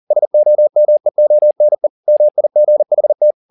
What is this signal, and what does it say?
Morse code, heard as a keyed tone, about 40 wpm, 600 Hz. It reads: SOMEONE MIGHT
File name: the old man and the sea 2 40wpm